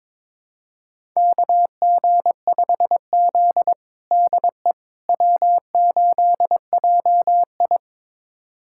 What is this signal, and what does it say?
Morse code, heard as a keyed tone, about 22 wpm, 700 Hz. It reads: KG5Z DE W8JI
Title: USA Callsign DE Callsign - 4 Characters - 22wpm